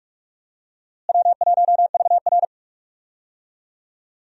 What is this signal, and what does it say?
Morse code, heard as a keyed tone, about 44 wpm, 700 Hz. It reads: W1VR
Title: USA Call Signs - 4 Characters - Repeated 2x - 45wpm